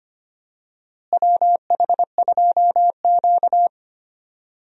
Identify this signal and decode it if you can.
Morse code, heard as a keyed tone, about 25 wpm, 700 Hz. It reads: WH2Q
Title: USA Call Signs - 4 Characters - Repeated 3x - 25wpm